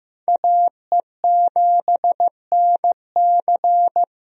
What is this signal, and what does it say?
Morse code, heard as a keyed tone, about 15 wpm, 700 Hz. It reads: AE7NC
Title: USA Call Signs - 5 Characters - Repeated 2x - 15wpm